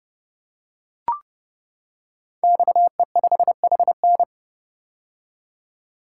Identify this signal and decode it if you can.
Morse code, heard as a keyed tone, about 30 wpm, 700 Hz. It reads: XE5HN
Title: Code Groups - 5 Letter:Numbers - 30wpm